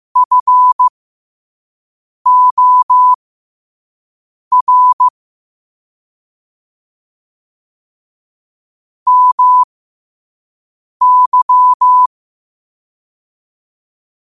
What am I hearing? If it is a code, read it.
Morse code, heard as a keyed tone, about 15 wpm, spaced as for 5 wpm, 1000 Hz. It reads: FOR MY
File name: qotdfiles / 5wpm